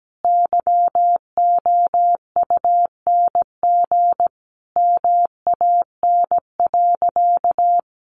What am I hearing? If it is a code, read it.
Morse code, heard as a keyed tone, about 17 wpm, 700 Hz. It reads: YOUNG MAN.
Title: Sentences from Top 1000 Words 17wpm